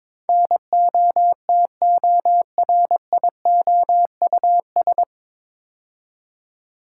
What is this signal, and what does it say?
Morse code, heard as a keyed tone, about 22 wpm, 700 Hz. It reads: NOTORIOUS
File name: 9 Letter Words - First 3 Letters Given - 22wpm